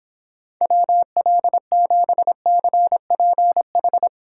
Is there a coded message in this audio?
Yes: Morse code from 0.6 to 4.1 s, heard as a keyed tone, about 29 wpm, 700 Hz. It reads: WL7CPH